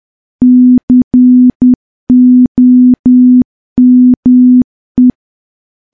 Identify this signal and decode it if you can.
Morse code, heard as a keyed tone, about 10 wpm, 250 Hz. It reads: COME
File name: morse / morse-training-10WPM